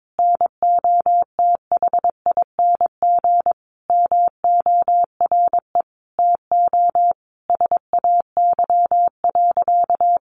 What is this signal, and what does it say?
Morse code, heard as a keyed tone, about 22 wpm, 700 Hz. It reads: NOTHING MORE TO SAY.
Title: Sentences from Top 500 Words 22wpm